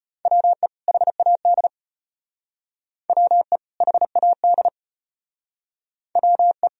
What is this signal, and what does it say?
Morse code, heard as a keyed tone, about 32 wpm, 700 Hz. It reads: WE HAD WE HAD WE